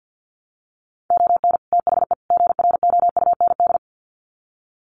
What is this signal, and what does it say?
Morse code, heard as a keyed tone, about 50 wpm, 700 Hz. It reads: ON THE GROUND
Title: Sets of 3 Words 50wpm